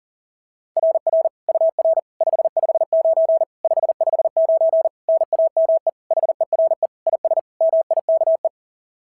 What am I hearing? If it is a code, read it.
Morse code, heard as a keyed tone, about 40 wpm, 690 Hz. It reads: RR UR 559 559 NAME HERE IS MIKE
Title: SOTA QSO Practice 40wpm